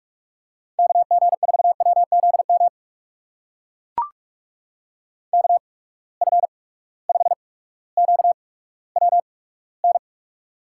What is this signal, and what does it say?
Morse code, heard as a keyed tone, about 44 wpm, spaced as for 19 wpm, 700 Hz. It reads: KG4WZM KF5QWN
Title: USA Call Signs - 6 Characters - 8x Character Spacing - 45wpm